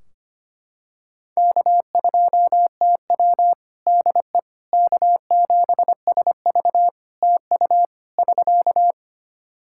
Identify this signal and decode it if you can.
Morse code, heard as a keyed tone, about 25 wpm, 700 Hz. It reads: K2TW DE K7SV TU <SK>
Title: K1USN SST QSO Practice 25wpm